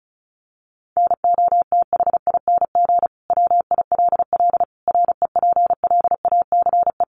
Morse code, heard as a keyed tone, about 34 wpm, 700 Hz. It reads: NOTHING WILL REPLACE
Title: Sentences from Top 1400 Words 35wpm